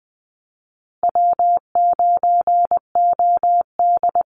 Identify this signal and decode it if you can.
Morse code, heard as a keyed tone, about 20 wpm, 700 Hz. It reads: W9OD
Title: Call Signs Repeated 2x - 20wpm